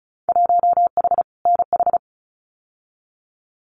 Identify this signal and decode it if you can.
Morse code, heard as a keyed tone, about 36 wpm, 700 Hz. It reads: 1H NH
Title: Winter Field Day Exchanges 35wpm